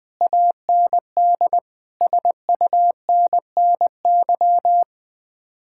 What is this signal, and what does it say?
Morse code, heard as a keyed tone, about 20 wpm, 700 Hz. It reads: AND SUNNY
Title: Speed-Racing Common QSO Phrases Encore - 17wpm